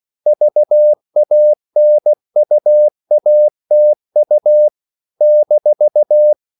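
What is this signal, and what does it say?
Morse code, heard as a keyed tone, about 16 wpm, 600 Hz. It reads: VANUATU -
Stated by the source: CW practice of national capitals at 16 words per minute, from ON6ZQ.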